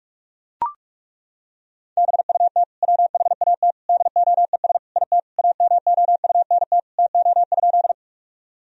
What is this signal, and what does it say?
Morse code, heard as a keyed tone, about 41 wpm, 700 Hz. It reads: BUT WHAT DOES IT AMOUNT TO?
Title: Sentences from Top 1000 Words - Review Part 4 - 45wpm